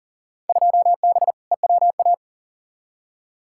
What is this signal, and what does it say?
Morse code, heard as a keyed tone, about 40 wpm, 700 Hz. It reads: 2B EWA
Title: Field Day Exchanges Encore - 40wpm